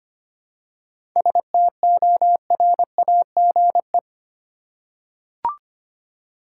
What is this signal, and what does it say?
Morse code, heard as a keyed tone, about 25 wpm, 700 Hz. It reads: STORAGE
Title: Seven Letter Words - 25wpm